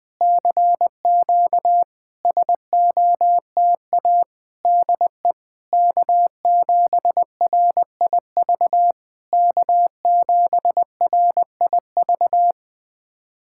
Morse code, heard as a keyed tone, about 20 wpm, 700 Hz. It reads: CQ SOTA DE K7RIV K7RIV
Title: SOTA QSO Practice 20wpm